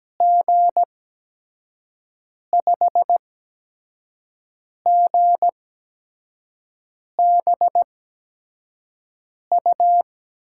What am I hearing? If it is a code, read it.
Morse code, heard as a keyed tone, about 17 wpm, spaced as for 5 wpm, 700 Hz. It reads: G5GBU
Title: USA Call Signs - 6 Characters - 8x Character Spacing - 17wpm